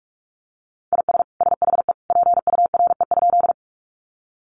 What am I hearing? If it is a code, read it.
Morse code, heard as a keyed tone, about 44 wpm, 700 Hz. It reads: IS SHE PURE?
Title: Sentences from Top 1500 Words 45wpm